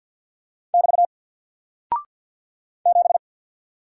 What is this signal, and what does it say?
Morse code, heard as a keyed tone, about 48 wpm, 700 Hz. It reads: <BT> 7
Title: Single Character (Letter:Number:Punctuation:Prosign) Alt 50wpm